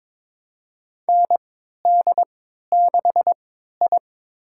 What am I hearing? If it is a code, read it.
Morse code, heard as a keyed tone, about 22 wpm, spaced as for 13 wpm, 700 Hz. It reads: ND6I
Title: USA Call Signs - 4 Characters - 3x Character Spacing - 22wpm